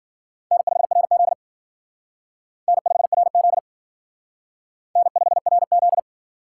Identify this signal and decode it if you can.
Morse code, heard as a keyed tone, about 52 wpm, 700 Hz. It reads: N5RZ N5RZ N5RZ